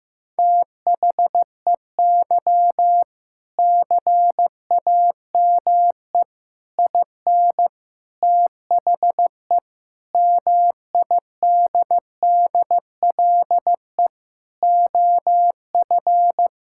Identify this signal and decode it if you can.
Morse code, heard as a keyed tone, about 15 wpm, 700 Hz. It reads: THEY CAME IN THE MIDDLE OF
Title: Sentences from Top 1000 Words - Review Part 6 - 15wpm